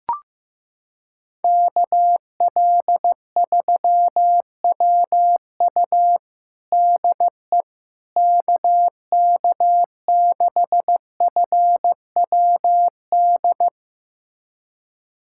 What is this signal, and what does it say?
Morse code, heard as a keyed tone, about 15 wpm, 700 Hz. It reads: KL3WU DE KK6FWD
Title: USA Callsign DE Callsign 15wpm